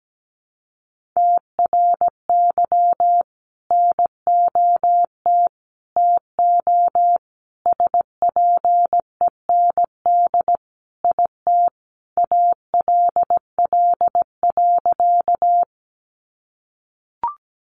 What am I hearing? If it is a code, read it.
Morse code, heard as a keyed tone, about 17 wpm, 700 Hz. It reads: TRY NOT TO SPEND IT ALL.
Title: Sentences from Top 800 Words 17wpm